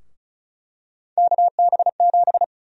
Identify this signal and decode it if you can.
Morse code, heard as a keyed tone, about 34 wpm, 700 Hz. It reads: KB7
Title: USA Call Sign Prefixes - 35wpm